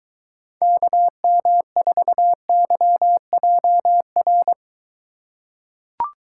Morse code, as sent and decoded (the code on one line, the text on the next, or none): -.- -- ....- -.-- .--- .-.
KM4YJR